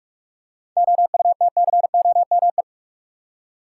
Morse code, heard as a keyed tone, about 46 wpm, 700 Hz. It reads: OUTCOME
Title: Seven Letter Words - 45wpm